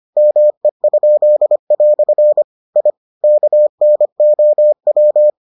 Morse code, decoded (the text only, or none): ME?" I KNOW